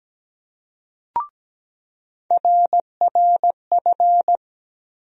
Morse code, heard as a keyed tone, about 17 wpm, 700 Hz. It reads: RRF
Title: USA Call Sign Postfixes - 3 Letters 17wpm